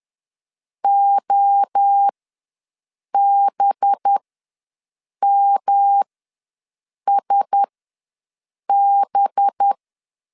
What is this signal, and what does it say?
Morse code, heard as a keyed tone, about 10 wpm, spaced as for 6 wpm, 800 Hz. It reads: OBMSB
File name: LEZIONI / 5 LBVC / 5 LB ws 10wpm 3616